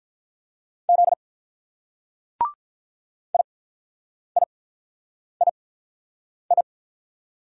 Morse code, heard as a keyed tone, about 48 wpm, spaced as for 9 wpm, 700 Hz. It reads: Z IIII